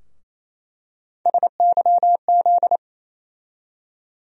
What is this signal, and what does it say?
Morse code, heard as a keyed tone, about 28 wpm, 700 Hz. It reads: SYZ
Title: USA Call Sign Postfixes - 3 Letters 28wpm